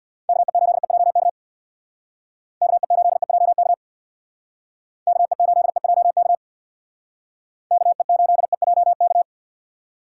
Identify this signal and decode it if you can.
Morse code, heard as a keyed tone, about 65 wpm, 700 Hz. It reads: KE8EJK KE8EJK KE8EJK KE8EJK